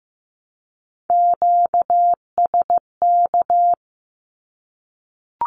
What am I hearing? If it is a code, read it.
Morse code, heard as a keyed tone, about 14 wpm, 700 Hz. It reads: QSK
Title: Q-Codes Repeated 3x 15wpm